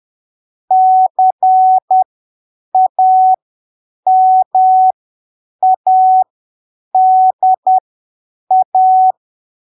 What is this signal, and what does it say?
Morse code, heard as a keyed tone, about 10 wpm, spaced as for 7 wpm, 750 Hz. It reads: CAMADA
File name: arquivos / texto / texto 11 10ppm